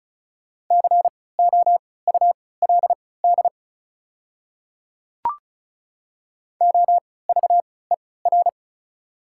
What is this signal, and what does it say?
Morse code, heard as a keyed tone, about 36 wpm, spaced as for 20 wpm, 690 Hz. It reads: COULD OVER
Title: 3x Character Spacing - Top 100 Words - 35wpm